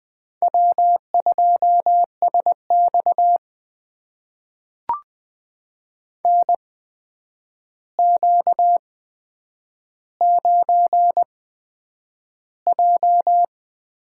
Morse code, heard as a keyed tone, about 20 wpm, spaced as for 9 wpm, 700 Hz. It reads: W2SX NQ9J
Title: USA Call Signs - 8x Character Spacing - 20wpm